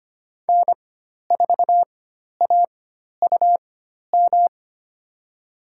Morse code, heard as a keyed tone, about 25 wpm, spaced as for 12 wpm, 690 Hz. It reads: N4AUM
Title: USA Call Signs - 4x Character Spacing - 25wpm